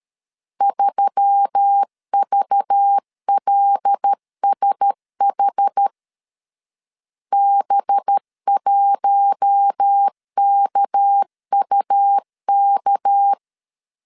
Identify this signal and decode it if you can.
Morse code, heard as a keyed tone, about 13 wpm, 800 Hz. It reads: 3VLSH B1KUK
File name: LEZIONI / 6 PKX0 / 6 PK 12wpm 2006